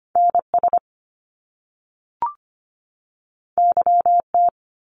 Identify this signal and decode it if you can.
Morse code, heard as a keyed tone, about 25 wpm, 700 Hz. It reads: NS YT
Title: Canadian Provinces and Territories 25wpm